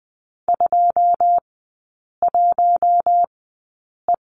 Morse code, heard as a keyed tone, about 21 wpm, spaced as for 8 wpm, 700 Hz. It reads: 21E
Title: 121 Lesson 20e - Number 3 - Three Characters 20wpm